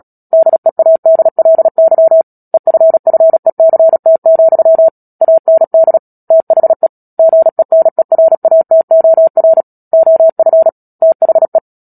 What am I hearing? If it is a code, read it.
Morse code, heard as a keyed tone, about 37 wpm, 650 Hz. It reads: DEADLY EFFECT, AND THE GENERATOR OF THE
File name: Book Two / 10-Chapter 10 - Epilogue - 36 WPM